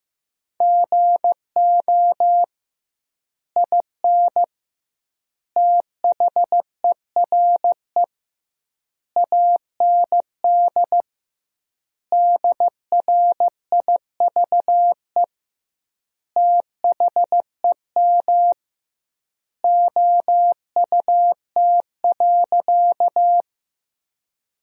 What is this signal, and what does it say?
Morse code, heard as a keyed tone, about 16 wpm, 700 Hz. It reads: GO IN THERE AND DRIVE THEM OUT.
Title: 2x Word-Spacing - Sentences from Top 500 Words - 15wpm